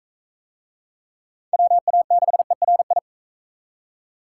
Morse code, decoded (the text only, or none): WA6ERI